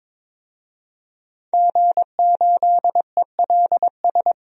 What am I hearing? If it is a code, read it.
Morse code, heard as a keyed tone, about 22 wpm, 700 Hz. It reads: G8ELS
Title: DXCC-100 Call Signs - 5 Characters - Repeated 2x - 22wpm